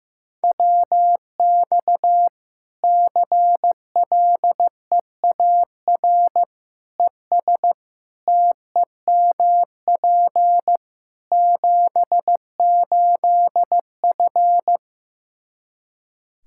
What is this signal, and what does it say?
Morse code, heard as a keyed tone, about 15 wpm, 700 Hz. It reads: WX CLEAR ES TEMP 78F
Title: Practice QSO - Exchange 2 - Info - 15wpm